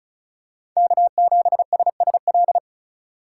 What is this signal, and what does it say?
Morse code, heard as a keyed tone, about 36 wpm, 700 Hz. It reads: K7SSL
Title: USA Call Signs - 5 Characters - 4x Character Spacing - 35wpm